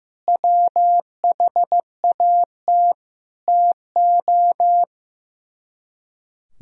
Morse code, heard as a keyed tone, about 15 wpm, 700 Hz. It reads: WHAT TO
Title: Speed-Racing - Sets of 4 Words 15wpm